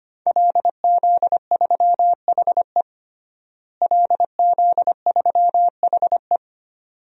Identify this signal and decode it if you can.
Morse code, heard as a keyed tone, about 25 wpm, 700 Hz. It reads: LZ3HE LZ3HE
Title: DXCC-100 Call Signs - 5 Characters - Repeated 2x - 25wpm